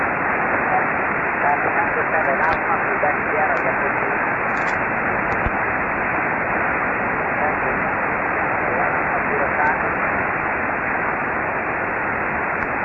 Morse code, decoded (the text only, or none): none